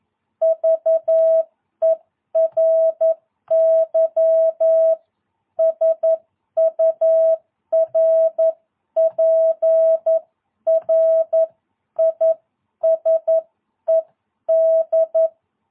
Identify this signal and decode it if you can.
Morse code, heard as a keyed tone, about 11 wpm, spaced as for 10 wpm, 660 Hz. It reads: VERY SURPRISED
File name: jh8cbh / 2022cw